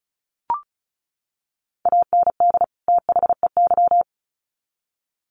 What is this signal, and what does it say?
Morse code, heard as a keyed tone, about 34 wpm, 700 Hz. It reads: AND THEY
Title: Sets of 2 Words 35wpm